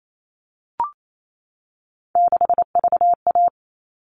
Morse code, heard as a keyed tone, about 28 wpm, 700 Hz. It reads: 6VA
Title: All Cut Numbers - 3-Digit 28wpm